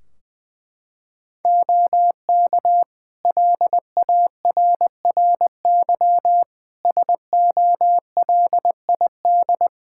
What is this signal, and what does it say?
Morse code, heard as a keyed tone, about 20 wpm, 700 Hz. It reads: OK LARRY SOLID
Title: Practice QSO - Exchange 3 - More Info - 20wpm